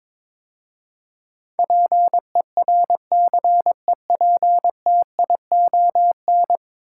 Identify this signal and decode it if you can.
Morse code, heard as a keyed tone, about 22 wpm, 700 Hz. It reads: PERCEPTION
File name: practice / 22wpm / 10 Letter Words - First 3 Letters Given - 22wpm